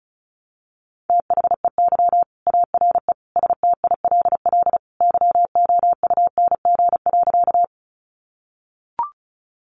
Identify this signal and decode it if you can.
Morse code, heard as a keyed tone, about 34 wpm, 700 Hz. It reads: THEY ARE STILL YOUNG.
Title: Sentences from Top 400 Words 35wpm